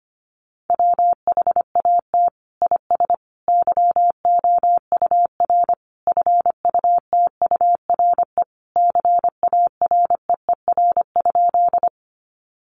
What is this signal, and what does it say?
Morse code, heard as a keyed tone, about 25 wpm, 700 Hz. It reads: WHAT IS YOUR FUTURE CAREER?
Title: Sentences from Top 1000 Words 25wpm